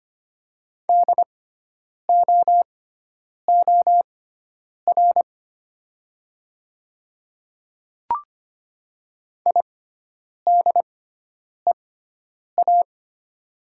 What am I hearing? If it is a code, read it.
Morse code, heard as a keyed tone, about 25 wpm, spaced as for 9 wpm, 700 Hz. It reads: DOOR IDEA
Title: 6x Character Spacing - Top 300 Words - 25wpm